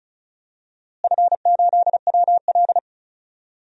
Morse code, heard as a keyed tone, about 36 wpm, 700 Hz. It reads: F8WL
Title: DXCC-100 Call Signs - 4 Characters - Repeated 2x - 35wpm